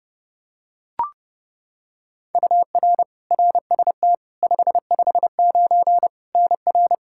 Morse code, heard as a keyed tone, about 30 wpm, 700 Hz. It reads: UR RST 559 NR
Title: Practice QSO - Exchange 1 - Beginning - 30wpm